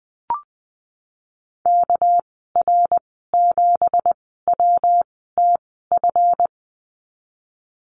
Morse code, heard as a keyed tone, about 20 wpm, spaced as for 15 wpm, 700 Hz. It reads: KR7WTF
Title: 219 Lesson 35g - Number 0 - Call Signs 20wpm